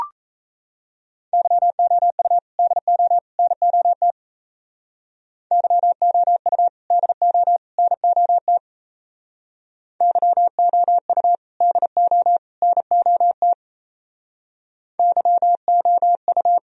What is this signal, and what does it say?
Morse code, heard as a keyed tone, about 34 wpm, 700 Hz. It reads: YOU DO NOT YOU DO NOT YOU DO NOT YOU
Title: Speed-Racing - Sets of 3 Words 28wpm